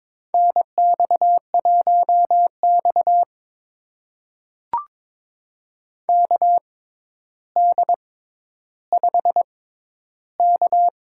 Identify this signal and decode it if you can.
Morse code, heard as a keyed tone, about 22 wpm, spaced as for 11 wpm, 700 Hz. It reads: NX1X KD5K
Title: USA Call Signs - 4 Characters - 6x Character Spacing - 22wpm